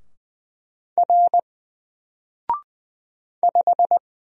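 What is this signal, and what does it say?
Morse code, heard as a keyed tone, about 20 wpm, 700 Hz. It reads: R 5
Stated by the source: Single letter and number Morse drill at 20 wpm, from N4LCD.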